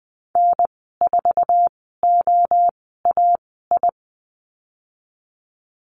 Morse code, heard as a keyed tone, about 19 wpm, spaced as for 14 wpm, 700 Hz. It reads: N4OAI
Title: 055 Lesson 10g - Letter r - Call Signs 20wpm